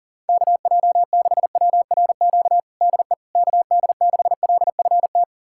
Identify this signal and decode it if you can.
Morse code, heard as a keyed tone, about 37 wpm, 700 Hz. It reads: KJ6WRQ DE KD6LFT
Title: USA Callsign DE Callsign - 6 Characters - 40wpm